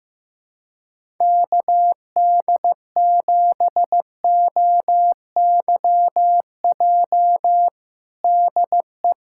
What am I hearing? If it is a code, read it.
Morse code, heard as a keyed tone, about 15 wpm, 700 Hz. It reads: KD7OYJ DE
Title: USA Callsign DE Callsign - 6 Characters - 15wpm